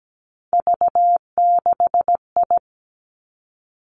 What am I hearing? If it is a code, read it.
Morse code, heard as a keyed tone, about 17 wpm, 700 Hz. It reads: V6I
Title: Three Challenging Letter-Number Combinations 17wpm